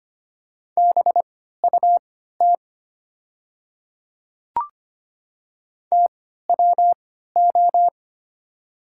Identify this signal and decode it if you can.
Morse code, heard as a keyed tone, about 25 wpm, spaced as for 14 wpm, 700 Hz. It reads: BUT TWO
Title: 3x Character Spacing - Top 200 Words - 25wpm